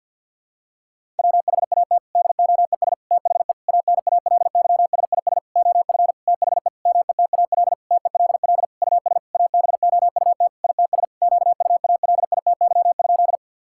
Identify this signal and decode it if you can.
Morse code, heard as a keyed tone, about 50 wpm, 700 Hz. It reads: WHAT DOES THE ANALYSIS OF THE METAL TELL US ABOUT ITS QUALITY?
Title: Sentences from Top 1000 Words - Review Part 6 - 50wpm